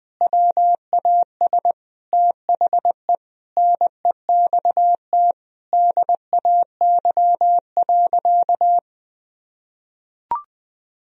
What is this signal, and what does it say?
Morse code, heard as a keyed tone, about 20 wpm, 700 Hz. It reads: WAS THE NEXT DAY.
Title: Sentences from Top 1000 Words - Review Part 1 - 20wpm